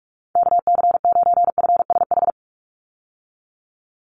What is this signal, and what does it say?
Morse code, heard as a keyed tone, about 46 wpm, 700 Hz. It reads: KC9FSH
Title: USA Call Signs 45wpm